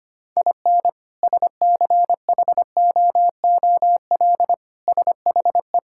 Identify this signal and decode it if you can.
Morse code, heard as a keyed tone, about 25 wpm, 700 Hz. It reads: IN SCHOOL SHE
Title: Sentences from Top 1000 Words - Review Part 5 - 25wpm